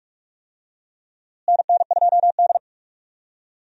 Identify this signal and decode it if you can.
Morse code, heard as a keyed tone, about 46 wpm, 700 Hz. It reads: NN2D
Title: USA Call Signs - 4 Characters - Repeated 3x - 45wpm